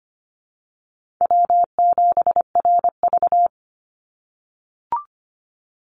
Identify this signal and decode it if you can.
Morse code, heard as a keyed tone, about 25 wpm, 700 Hz. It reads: W7RV
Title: NA SOTA Chaser Callsigns 25wpm